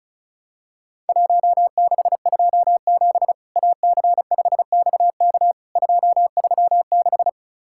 Morse code, heard as a keyed tone, about 36 wpm, 700 Hz. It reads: 1627 AC5XK 236
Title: Worked All Europe DX Contest - QTC Exchange - 35wpm